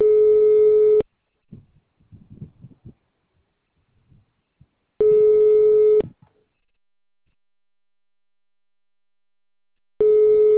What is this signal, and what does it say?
Ringback tone (Europe, 425 Hz).